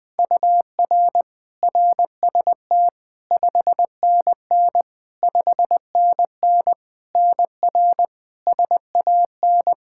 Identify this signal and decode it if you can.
Morse code, heard as a keyed tone, about 20 wpm, 700 Hz. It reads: UR RST 5NN 5NN NR SAN
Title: Practice QSO - Exchange 1 - Beginning - 20wpm